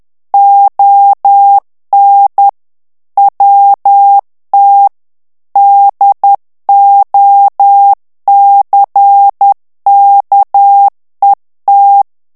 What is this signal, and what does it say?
Morse code, heard as a keyed tone, about 10 wpm, 800 Hz. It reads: ON WT DOCKET
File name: ANGLAIS texte clair 3 vitesse 10 mots minute